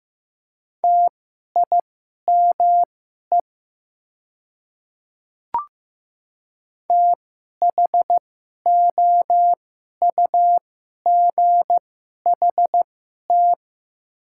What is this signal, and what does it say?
Morse code, heard as a keyed tone, about 15 wpm, spaced as for 11 wpm, 700 Hz. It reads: TIME THOUGHT
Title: 2x Character Spacing - Top 500 Words - 15wpm